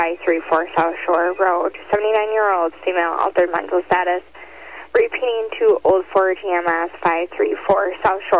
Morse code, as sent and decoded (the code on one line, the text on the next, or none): none